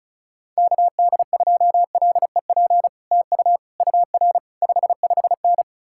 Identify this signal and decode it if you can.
Morse code, heard as a keyed tone, about 46 wpm, spaced as for 41 wpm, 700 Hz. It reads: KD2LEP TU UR 55N
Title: Speed-Racing Common QSO Phrases Encore - 35wpm